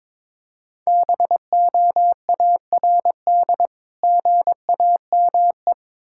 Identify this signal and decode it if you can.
Morse code, heard as a keyed tone, about 22 wpm, 690 Hz. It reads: BOARD GAME
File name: Difficult Words with Hints - 22wpm